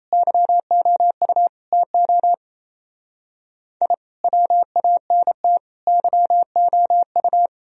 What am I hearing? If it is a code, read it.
Morse code, heard as a keyed tone, about 35 wpm, 700 Hz. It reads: YOU TO I WANT YOU